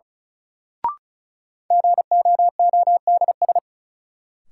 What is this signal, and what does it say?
Morse code, heard as a keyed tone, about 34 wpm, 700 Hz. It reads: GOODS
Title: Sentences from Top 1000 Words - Review Part 6 - New Word Drill - 35wpm